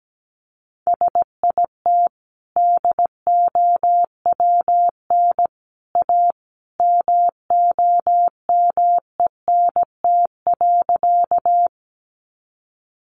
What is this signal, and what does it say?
Morse code, heard as a keyed tone, about 17 wpm, 700 Hz. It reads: SIT DOWN A MOMENT.
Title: Sentences from Top 1000 Words 17wpm